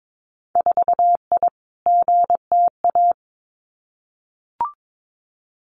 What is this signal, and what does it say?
Morse code, heard as a keyed tone, about 22 wpm, 700 Hz. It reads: 4I GTA
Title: Winter Field Day Exchanges Encore 22wpm